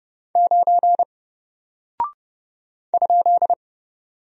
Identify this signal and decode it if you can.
Morse code, heard as a keyed tone, about 31 wpm, 700 Hz. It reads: 9 ?
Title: Single Character (Letter:Number:Punctuation:Prosign) Alt 30wpm